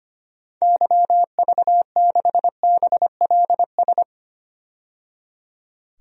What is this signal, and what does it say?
Morse code, heard as a keyed tone, about 25 wpm, 700 Hz. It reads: YV6BLS
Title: DXCC-100 Call Signs - 6 Characters - 25wpm